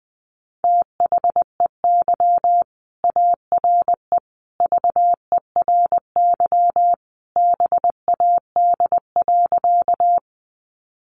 Morse code, heard as a keyed tone, about 20 wpm, 700 Hz. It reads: THEY ARE VERY BAD.
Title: Sentences from Top 700 Words 20wpm